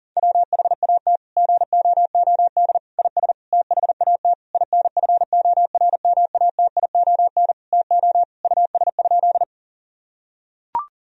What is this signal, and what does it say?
Morse code, heard as a keyed tone, about 40 wpm, 700 Hz. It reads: WHAT GOOD IS THAT INFORMATION TO US?
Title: Sentences from Top 1000 Words - Review Part 3 - 40wpm